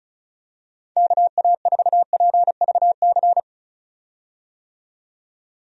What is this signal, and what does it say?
Morse code, heard as a keyed tone, about 34 wpm, 700 Hz. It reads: KA4PVC
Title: USA Call Signs - 6 Characters - 35wpm